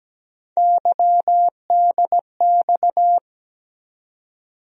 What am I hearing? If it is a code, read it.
Morse code, heard as a keyed tone, about 17 wpm, 700 Hz. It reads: YDX
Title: USA Call Sign Postfixes - 3 Letters 17wpm